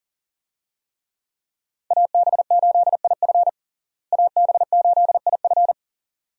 Morse code, heard as a keyed tone, about 40 wpm, 700 Hz. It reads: AB8IF AB8IF
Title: USA Call Signs - 5 Characters - Repeated 2x - 40wpm